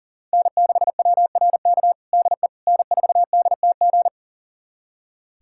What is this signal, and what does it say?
Morse code, heard as a keyed tone, about 37 wpm, 700 Hz. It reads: N6WRK DE N4DTG
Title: USA Callsign DE Callsign 40wpm